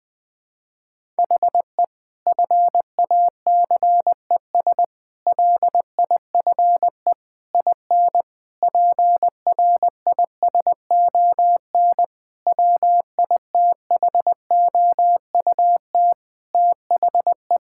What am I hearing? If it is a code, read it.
Morse code, heard as a keyed tone, about 20 wpm, 700 Hz. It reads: HE FACES LIFE IN PRISON WITHOUT THE
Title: Sentences with 8-Word Phrases - 20wpm